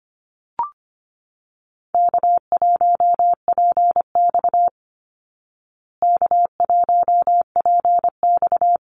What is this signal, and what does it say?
Morse code, heard as a keyed tone, about 25 wpm, 700 Hz. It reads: K1PX K1PX
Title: Call Signs Repeated 3x - 25wpm